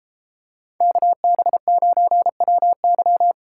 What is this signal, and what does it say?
Morse code, heard as a keyed tone, about 29 wpm, 700 Hz. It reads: KB9WY